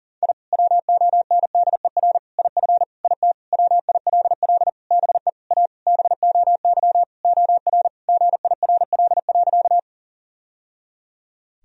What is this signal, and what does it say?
Morse code, heard as a keyed tone, about 40 wpm, 700 Hz. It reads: I WONDER IF IT WILL BE A BOY OR GIRL.
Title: Sentences from Top 1000 Words - Review Part 3 - 40wpm